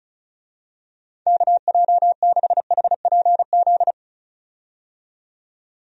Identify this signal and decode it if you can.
Morse code, heard as a keyed tone, about 36 wpm, 700 Hz. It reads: KJ6HPZ
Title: USA Call Signs - 6 Characters - 35wpm